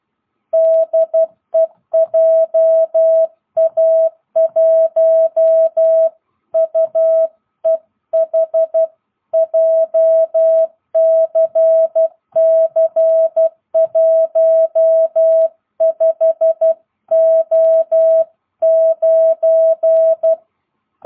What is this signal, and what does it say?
Morse code, heard as a keyed tone, about 12 wpm, 660 Hz. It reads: DE JA1UEH JCC15O9
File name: jh8cbh / 2022cw